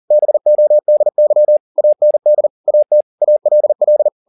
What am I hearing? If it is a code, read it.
Morse code, heard as a keyed tone, about 40 wpm, 600 Hz. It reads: BODY AND AT ALL
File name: the old man and the sea 2 40wpm